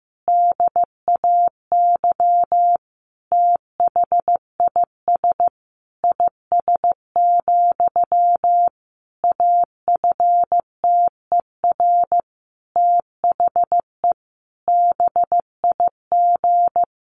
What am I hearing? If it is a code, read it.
Morse code, heard as a keyed tone, about 14 wpm, 700 Hz. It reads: DAY THIS IS, AFTER THE BIG
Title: Sentences from Top 1500 Words 15wpm